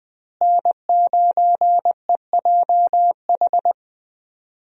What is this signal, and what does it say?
Morse code, heard as a keyed tone, about 20 wpm, 700 Hz. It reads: N9EJH
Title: USA Call Signs - 5 Characters - Repeated 3x - 20wpm